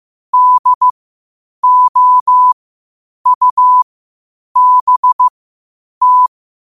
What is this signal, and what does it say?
Morse code, heard as a keyed tone, about 15 wpm, spaced as for 8 wpm, 1000 Hz. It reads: DOUBT
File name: qotdfiles / 8wpm